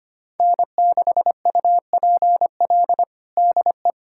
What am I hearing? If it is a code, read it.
Morse code, heard as a keyed tone, about 25 wpm, 700 Hz. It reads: N6UPL DE
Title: USA Callsign DE Callsign - 5 Characters - 25wpm